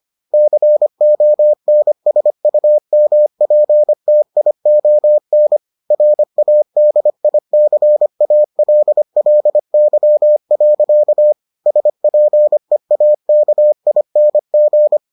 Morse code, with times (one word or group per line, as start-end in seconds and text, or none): 0.3-5.6 CONSUMPTION
5.9-11.3 RADICALLY.
11.7-15.0 SPEAKING